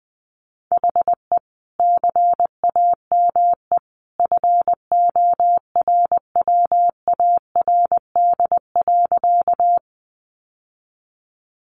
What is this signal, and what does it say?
Morse code, heard as a keyed tone, about 20 wpm, 700 Hz. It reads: HE CAME FORWARD.
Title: Sentences from Top 1300 Words 20wpm